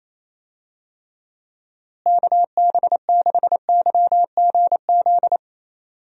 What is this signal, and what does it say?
Morse code, heard as a keyed tone, about 27 wpm, 700 Hz. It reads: KB6YGZ
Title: USA Call Signs - 6 Characters - Repeated 2x - 28wpm